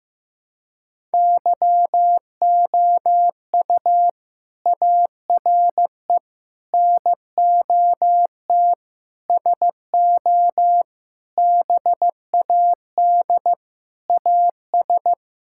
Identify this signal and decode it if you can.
Morse code, heard as a keyed tone, about 15 wpm, 700 Hz. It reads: YOU ARE NOT SO BAD AS
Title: Sentences from Top 1000 Words - Review Part 6 - 15wpm